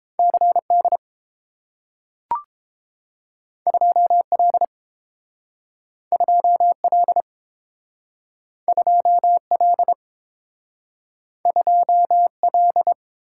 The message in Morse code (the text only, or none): CD 2L 2L 2L 2L